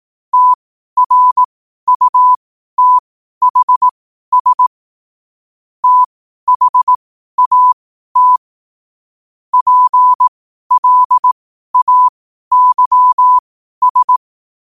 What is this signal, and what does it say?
Morse code, heard as a keyed tone, about 18 wpm, spaced as for 12 wpm, 1000 Hz. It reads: TRUTHS THAT PLAYS
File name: qotdfiles / 12wpm